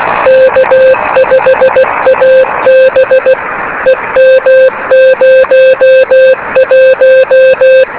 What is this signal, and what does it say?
Morse code, heard as a keyed tone, about 16 wpm, 520 Hz, with radio noise. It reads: K5AB EM01
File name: beacons / k5ab